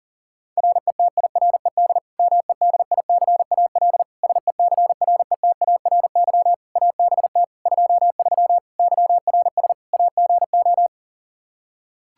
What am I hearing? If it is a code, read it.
Morse code, heard as a keyed tone, about 40 wpm, 700 Hz. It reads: RETIRED MEDICAL SECRETARY ABT 23 YRS AGO
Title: K1USN SST QSO Practice 40wpm